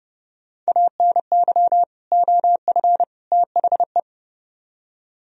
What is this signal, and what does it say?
Morse code, heard as a keyed tone, about 38 wpm, 700 Hz. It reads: ANY OF THE